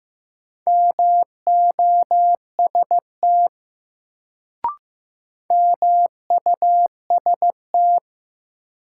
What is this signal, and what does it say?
Morse code, heard as a keyed tone, about 15 wpm, 700 Hz. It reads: MOST MUST
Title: Sentences from Top 200 Words - New Word Drill - 15wpm